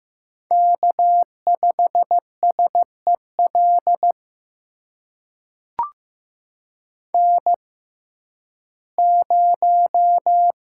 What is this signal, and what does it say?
Morse code, heard as a keyed tone, about 15 wpm, spaced as for 11 wpm, 700 Hz. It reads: K5SEL N0
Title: USA Call Signs - 6x Character Spacing - 15wpm